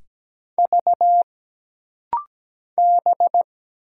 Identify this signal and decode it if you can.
Morse code, heard as a keyed tone, about 17 wpm, 700 Hz. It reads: V B
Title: Single Letters - Rapid-Fire 17wpm